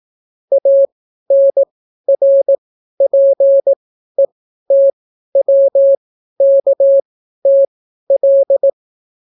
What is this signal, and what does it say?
Morse code, heard as a keyed tone, about 18 wpm, spaced as for 12 wpm, 550 Hz. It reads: ANRPETWKTL